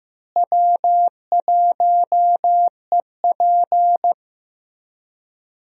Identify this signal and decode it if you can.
Morse code, heard as a keyed tone, about 16 wpm, 700 Hz. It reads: W1EP
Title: USA Call Signs - 4 Characters - Repeated 3x - 15wpm